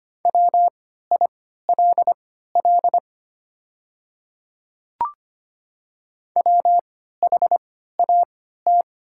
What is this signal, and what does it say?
Morse code, heard as a keyed tone, about 25 wpm, spaced as for 14 wpm, 700 Hz. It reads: WILL WHAT
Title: 3x Character Spacing - Top 100 Words - 25wpm